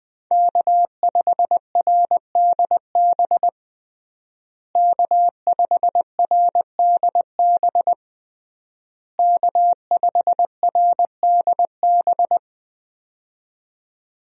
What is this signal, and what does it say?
Morse code, heard as a keyed tone, about 21 wpm, 700 Hz. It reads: K5RDB K5RDB K5RDB
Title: USA Call Signs - 5 Characters - Repeated 3x - 20wpm